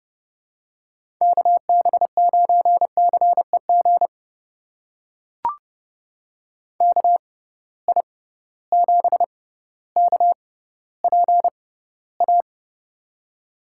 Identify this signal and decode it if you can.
Morse code, heard as a keyed tone, about 30 wpm, spaced as for 15 wpm, 700 Hz. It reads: KB9CEG KI7KPA
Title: USA Call Signs - 6 Characters - 6x Character Spacing - 30wpm